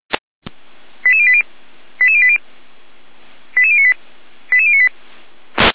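Five-tone selective call (ZVEI), heard as quick runs of five tones, tones of about 70 ms. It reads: 88080 88080 88098 88098
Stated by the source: ZVEI five-tone selective call 88571 received on an SDR.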